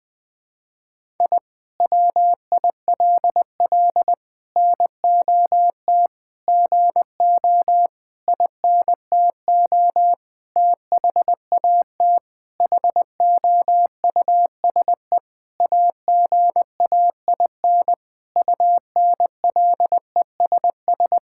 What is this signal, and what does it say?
Morse code, heard as a keyed tone, about 20 wpm, 700 Hz. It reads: I WILL NOT GO INTO THAT HOUSE AGAIN UNLESS
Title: Sentences from Top 1000 Words - Review Part 6 - 20wpm